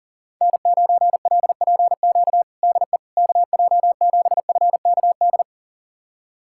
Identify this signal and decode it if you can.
Morse code, heard as a keyed tone, about 40 wpm, 700 Hz. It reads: N9LPQ DE KJ7FKD
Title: USA Callsign DE Callsign 40wpm